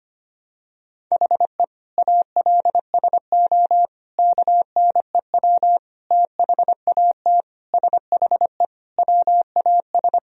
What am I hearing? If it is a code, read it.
Morse code, heard as a keyed tone, about 25 wpm, 700 Hz. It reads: HE ALSO KNEW THAT SHE WAS
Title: Sentences from Top 1000 Words - Review Part 1 - 25wpm